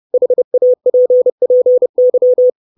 Morse code, heard as a keyed tone, about 30 wpm, 500 Hz. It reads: HAPPY
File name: 632 Words 30WPM